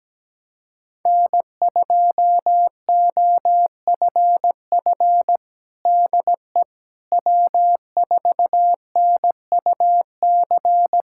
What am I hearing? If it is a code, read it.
Morse code, heard as a keyed tone, about 17 wpm, 700 Hz. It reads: N2OFF DE W4NUC
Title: USA Callsign DE Callsign - 5 Characters - 17wpm